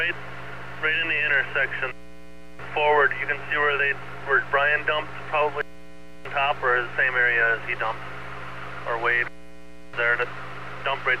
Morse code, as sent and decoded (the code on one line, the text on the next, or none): none